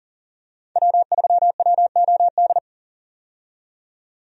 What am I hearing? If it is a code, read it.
Morse code, heard as a keyed tone, about 37 wpm, 700 Hz. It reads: W3WOD
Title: USA Call Signs - 5 Characters - 40wpm